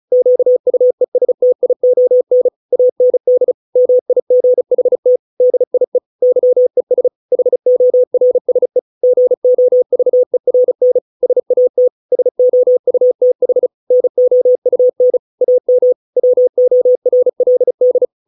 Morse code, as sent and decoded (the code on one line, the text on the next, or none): --.- ..- . ... - .. --- -. / .- -. -.. / -- .. --. .... - / -.. .. / -.-- . ... / .... --- .-. ... / --. --- ...- . .-. -. / ... .- - / ... --- ..- - .... / -. --- ..- -. / .- -- / .-- --- .-. .-.. -..
QUESTION AND MIGHT DIE YES HORSE GOVERN SAT SOUTH NOUN AM WORLD